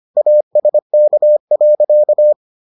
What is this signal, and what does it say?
Morse code, heard as a keyed tone, about 25 wpm, 600 Hz. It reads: ASK.